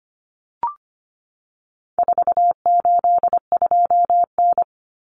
Morse code, heard as a keyed tone, about 25 wpm, 700 Hz. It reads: 482N
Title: Common Cut Numbers - 4-Digit 25wpm